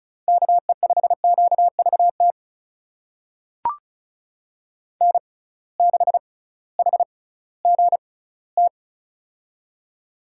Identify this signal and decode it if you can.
Morse code, heard as a keyed tone, about 36 wpm, spaced as for 19 wpm, 700 Hz. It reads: KE5QVT N6HGT
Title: USA Call Signs - 6x Character Spacing - 35wpm